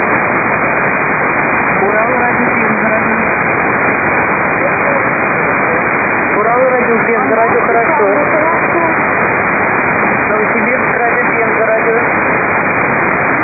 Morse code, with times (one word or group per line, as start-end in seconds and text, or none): none